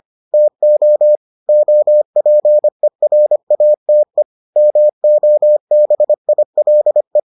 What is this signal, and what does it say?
Morse code, heard as a keyed tone, about 25 wpm, 600 Hz. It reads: TO OPERATE MOBILE